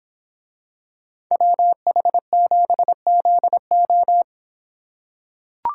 Morse code, heard as a keyed tone, about 25 wpm, 700 Hz. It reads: WH7ZO